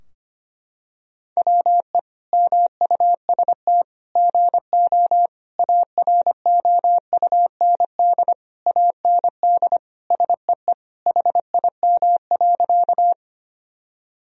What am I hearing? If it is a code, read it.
Morse code, heard as a keyed tone, about 25 wpm, 700 Hz. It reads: WE MUST GO AROUND AND SEE HIM.
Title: Sentences from Top 1000 Words - Review Part 3 - 25wpm